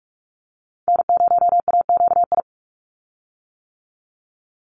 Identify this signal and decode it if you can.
Morse code, heard as a keyed tone, about 44 wpm, 690 Hz. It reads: N0AQI
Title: USA Call Signs 45wpm